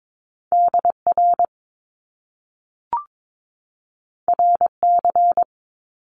Morse code, heard as a keyed tone, about 22 wpm, 700 Hz. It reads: DR RC
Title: Top Two Letter Combinations in Words 22wpm